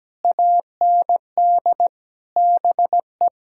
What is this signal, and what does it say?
Morse code, heard as a keyed tone, about 17 wpm, 700 Hz. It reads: AND BE
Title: Sentences from Top 1000 Words - Review Part 5 - 17wpm